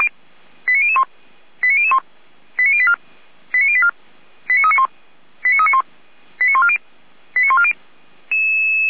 Five-tone selective call (ZVEI), heard as quick runs of five tones, tones of about 70 ms. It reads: 89001 89001 89084 89084 89391 89391 89140 89140
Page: /455/SDR/ZVEI/89035/